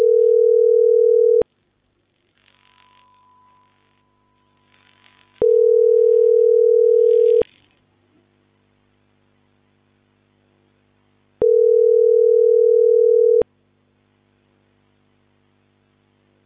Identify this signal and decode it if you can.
Ringback tone (North America), 440 + 480 Hz.